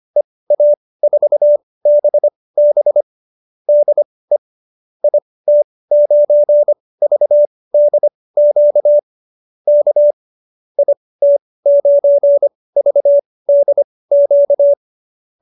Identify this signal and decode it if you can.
Morse code, heard as a keyed tone, about 25 wpm, spaced as for 18 wpm, 590 Hz. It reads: EA4BB DE IT9VDQ K IT9VDQ